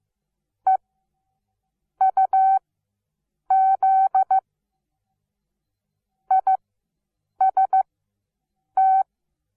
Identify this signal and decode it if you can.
Morse code, heard as a keyed tone, about 15 wpm, spaced as for 7 wpm, 760 Hz. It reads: EUZ IST
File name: Onlinelehrgang / morsen / DJ4UF Morsen17